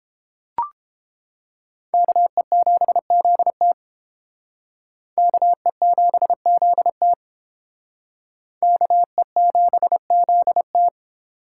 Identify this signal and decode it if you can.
Morse code, heard as a keyed tone, about 30 wpm, 700 Hz. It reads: KE7ZT KE7ZT KE7ZT